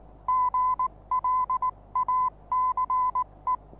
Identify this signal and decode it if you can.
Morse code, heard as a keyed tone, about 19 wpm, 1010 Hz. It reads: GLACE
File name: repos / Morsekurs / DJ4UF Morsen26